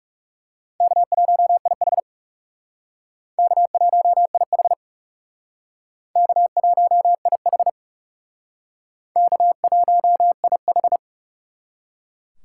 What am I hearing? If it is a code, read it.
Morse code, heard as a keyed tone, about 39 wpm, 700 Hz. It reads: K1IH K1IH K1IH K1IH